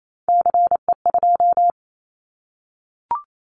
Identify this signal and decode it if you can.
Morse code, heard as a keyed tone, about 28 wpm, 700 Hz. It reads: CE2
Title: Callsign Prefixes for Top 100 DXCC - 28wpm